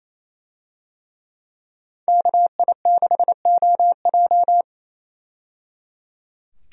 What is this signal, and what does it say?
Morse code, heard as a keyed tone, about 28 wpm, 700 Hz. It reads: KI6OJ
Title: USA Call Signs - 5 Characters - 28wpm